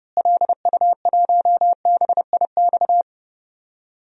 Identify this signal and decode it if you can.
Morse code, heard as a keyed tone, about 30 wpm, 700 Hz. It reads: LU1BIX
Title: DXCC-100 Call Signs - 6 Characters - Repeated 3x - 30wpm